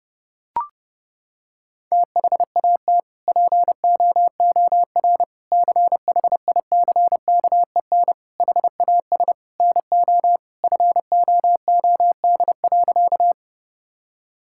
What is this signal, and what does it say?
Morse code, heard as a keyed tone, about 30 wpm, 700 Hz. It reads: THAT POOR CHICKEN HAS NO FOOD.
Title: Sentences from Top 1000 Words - Review Part 5 - 30wpm